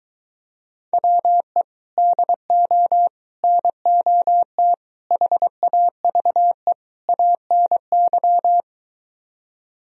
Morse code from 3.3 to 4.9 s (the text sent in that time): NOT